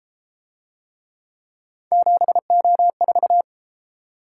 Morse code, heard as a keyed tone, about 33 wpm, 700 Hz. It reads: 7O4